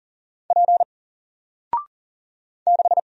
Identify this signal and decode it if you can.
Morse code, heard as a keyed tone, about 41 wpm, 700 Hz. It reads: P 6
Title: Single Character (Letter:Number:Punctuation:Prosign) 40wpm